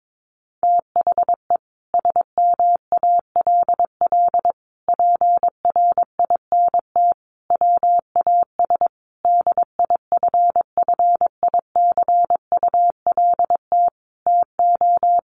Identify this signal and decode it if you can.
Morse code, heard as a keyed tone, about 22 wpm, 700 Hz. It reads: THE SMALL PRINT WAS DIFFICULT TO
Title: Sentences from Top 1400 Words 22wpm